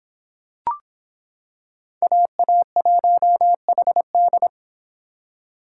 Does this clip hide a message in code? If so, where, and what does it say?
Yes: Morse code from 2.0 to 4.5 s, heard as a keyed tone, about 26 wpm, 700 Hz. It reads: AA1HD